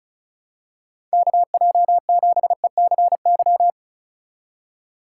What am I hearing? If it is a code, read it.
Morse code, heard as a keyed tone, about 36 wpm, 700 Hz. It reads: KJ7ECY
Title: USA Call Signs - 2x Character Spacing - 35wpm